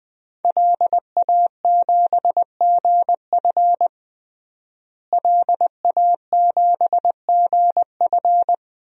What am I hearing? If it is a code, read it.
Morse code, heard as a keyed tone, about 20 wpm, 700 Hz. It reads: LA7GF LA7GF
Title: DXCC-100 Call Signs - 5 Characters - Repeated 3x - 20wpm